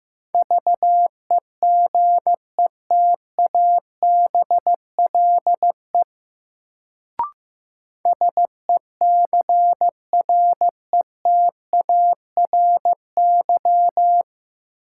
Morse code, heard as a keyed tone, about 15 wpm, 700 Hz. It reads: VEGETABLE SECRETARY
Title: Sentences from Top 1300 Words - New Word Drill - 15wpm